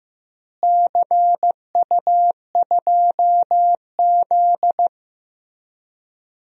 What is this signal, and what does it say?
Morse code, heard as a keyed tone, about 15 wpm, 700 Hz. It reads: CU2Z
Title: DXCC-100 Call Signs - 4 Characters - Repeated 2x - 15wpm